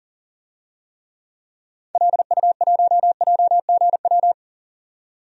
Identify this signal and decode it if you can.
Morse code, heard as a keyed tone, about 40 wpm, 700 Hz. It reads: LU1JGW
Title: DXCC-100 Call Signs - 6 Characters - 40wpm